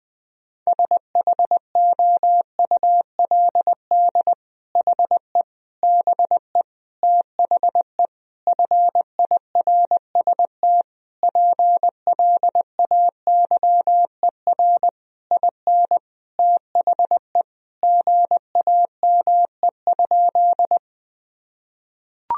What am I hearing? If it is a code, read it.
Morse code, heard as a keyed tone, about 20 wpm, 700 Hz. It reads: SHOULD HE BE THE FIRST PLAYER IN THE GAME?
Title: Sentences from Top 1000 Words - Review Part 4 - 20wpm